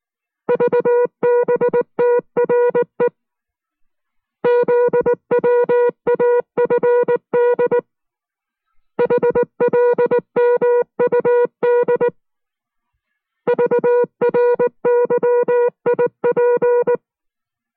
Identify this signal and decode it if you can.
Morse code, heard as a keyed tone, about 19 wpm, 480 Hz. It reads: VBTRE ZWAFD HLMUD VRYIP